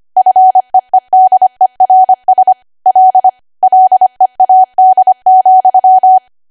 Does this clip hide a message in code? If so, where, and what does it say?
Yes: Morse code from 0.2 to 6.2 s, heard as a keyed tone, about 25 wpm, 750 Hz. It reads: FEEDERS L LEAD,